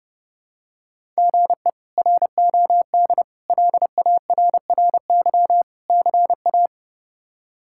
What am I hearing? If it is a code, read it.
Morse code, heard as a keyed tone, about 31 wpm, 700 Hz. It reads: GE ROD LARRY CA
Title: K1USN SST QSO Practice - 30wpm